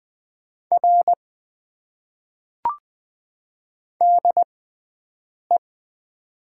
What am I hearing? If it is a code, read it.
Morse code, heard as a keyed tone, about 20 wpm, spaced as for 7 wpm, 700 Hz. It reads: R DE
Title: Top 90 QSO Elements - 6x Character Spacing - 20wpm